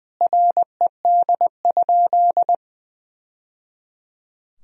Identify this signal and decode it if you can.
Morse code, heard as a keyed tone, about 21 wpm, 700 Hz. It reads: RED?
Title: Sentences from Top 1000 Words - Review Part 2 - 20wpm